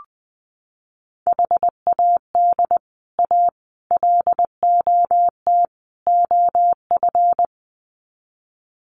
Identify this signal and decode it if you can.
Morse code, heard as a keyed tone, about 20 wpm, 700 Hz. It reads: HAD A LOT OF
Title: Sets of 4 Words 20wpm